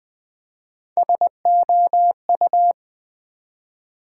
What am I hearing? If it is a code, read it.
Morse code, heard as a keyed tone, about 20 wpm, 700 Hz. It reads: SOU
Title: Build A Word - 1 Letter At A Time - 5-Letter Words - 20wpm